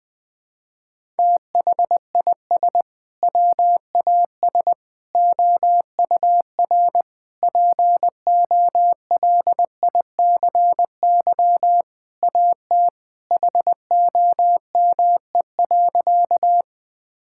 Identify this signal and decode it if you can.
Morse code, heard as a keyed tone, about 20 wpm, 700 Hz. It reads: THIS WAS OUR POLICY AT HOME.
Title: Sentences from Top 1000 Words - Review Part 5 - 20wpm